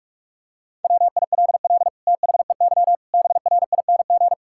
Morse code, heard as a keyed tone, about 41 wpm, 700 Hz. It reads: WILL THEY BRING
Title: Sentences from Top 1000 Words - Review Part 6 - 45wpm